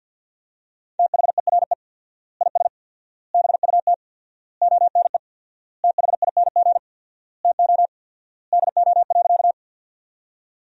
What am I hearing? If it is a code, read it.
Morse code, heard as a keyed tone, about 50 wpm, 700 Hz. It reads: THERE IS BUT ONE THING TO DO.
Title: 4x Word-Spacing - Sentences from Top 100 Words - 50wpm